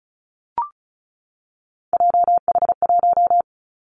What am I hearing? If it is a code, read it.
Morse code, heard as a keyed tone, about 34 wpm, 700 Hz. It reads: JH1
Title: Callsign Prefixes for Top 25 DXCC - 35wpm